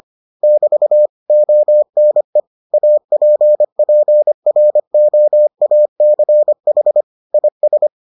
Morse code, heard as a keyed tone, about 25 wpm, 600 Hz. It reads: <BT> ONE APPROACH IS